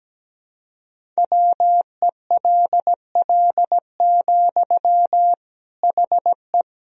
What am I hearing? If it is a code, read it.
Morse code, heard as a keyed tone, about 17 wpm, 700 Hz. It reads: WELL, HE
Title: Sentences from Top 1000 Words - Review Part 4 - 17wpm